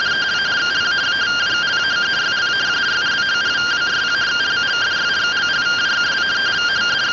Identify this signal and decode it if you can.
Two-tone teleprinter signal, about 170 Hz shift.
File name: findings / CT 205 ax mips, iDose 3 0005.gen-37 rtty rx